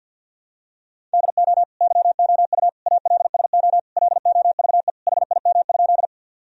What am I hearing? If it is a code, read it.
Morse code, heard as a keyed tone, about 52 wpm, 700 Hz. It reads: DO YOU ALSO LOVE HIM?
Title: Sentences from Top 1000 Words - Review Part 1 - 50wpm